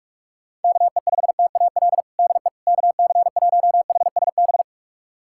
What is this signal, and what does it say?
Morse code, heard as a keyed tone, about 46 wpm, 700 Hz. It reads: KE5TAL DE KC1HSB
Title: USA Callsign DE Callsign - 6 Characters - 45wpm